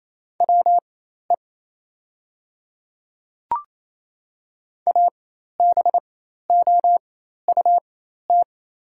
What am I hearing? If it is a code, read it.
Morse code, heard as a keyed tone, about 28 wpm, spaced as for 13 wpm, 700 Hz. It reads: WE ABOUT
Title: 4x Character Spacing - Top 100 Words - 28wpm